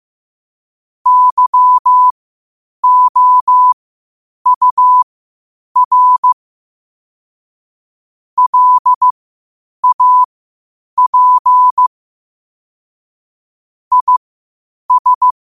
Morse code, heard as a keyed tone, about 15 wpm, spaced as for 8 wpm, 1000 Hz. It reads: YOUR LAP IS
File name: qotdfiles / 8wpm